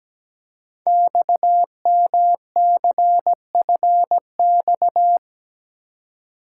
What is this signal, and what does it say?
Morse code, heard as a keyed tone, about 17 wpm, 700 Hz. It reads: XMCFX
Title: Code Groups - 5 Letters - 17wpm